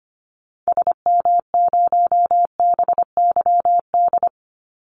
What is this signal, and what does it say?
Morse code, heard as a keyed tone, about 25 wpm, 700 Hz. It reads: SM0BYD
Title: Call Signs Repeated 2x - 25wpm